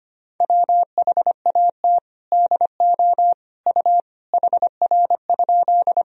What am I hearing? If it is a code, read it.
Morse code, heard as a keyed tone, about 25 wpm, 700 Hz. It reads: WHAT DO U HR?
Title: Sentences from Top 300 Words and Abbreviations 25wpm